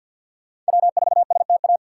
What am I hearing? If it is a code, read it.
Morse code, heard as a keyed tone, about 53 wpm, 700 Hz. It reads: W4STA
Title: USA Call Signs - 5 Characters - 6x Character Spacing - 50wpm